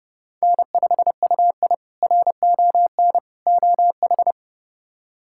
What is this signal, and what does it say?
Morse code, heard as a keyed tone, about 32 wpm, 700 Hz. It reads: N5UI RON OH
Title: K1USN SST QSO Practice - 30wpm